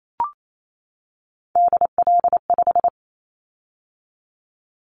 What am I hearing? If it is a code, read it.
Morse code, heard as a keyed tone, about 28 wpm, 700 Hz. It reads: DL5
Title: Callsign Prefixes for Top 50 DXCC - 28wpm